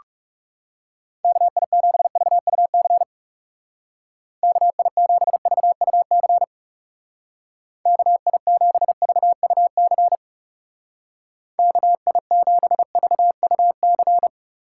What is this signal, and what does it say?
Morse code, heard as a keyed tone, about 37 wpm, 700 Hz. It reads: KI7VUC KI7VUC KI7VUC KI7VUC
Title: Speed Racing - USA 2x3 Call Signs 30wpm